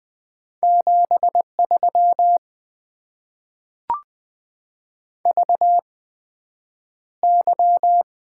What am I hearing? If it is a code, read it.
Morse code, heard as a keyed tone, about 20 wpm, spaced as for 9 wpm, 700 Hz. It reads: 73 VY
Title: Top 90 QSO Elements - 8x Character Spacing - 20wpm